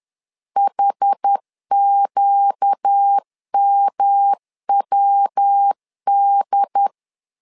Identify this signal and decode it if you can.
Morse code, heard as a keyed tone, about 10 wpm, 800 Hz. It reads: HQMWD
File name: LEZIONI / 4 Q 10wpm 2210